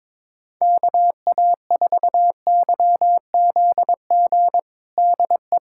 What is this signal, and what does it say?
Morse code, heard as a keyed tone, about 22 wpm, 700 Hz. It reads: KA4YZG DE
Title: USA Callsign DE Callsign - 6 Characters - 22wpm